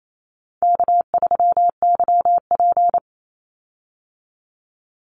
Morse code, heard as a keyed tone, about 28 wpm, 700 Hz. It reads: K3YP
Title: Call Signs - 28wpm